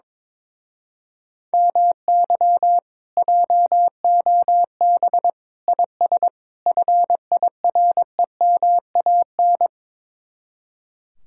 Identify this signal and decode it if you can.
Morse code, heard as a keyed tone, about 22 wpm, 700 Hz. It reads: MY JOB IS FIREMAN
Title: Speed-Racing Common QSO Phrases Encore - 22wpm